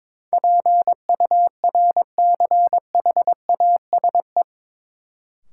Morse code, heard as a keyed tone, about 22 wpm, 700 Hz. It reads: PURCHASE
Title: Sentences from Top 800 Words - New Word Drill - 22wpm